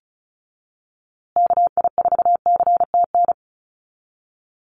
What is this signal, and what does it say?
Morse code, heard as a keyed tone, about 34 wpm, 700 Hz. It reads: KI4CTN
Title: USA Call Signs Repeated 2x 35wpm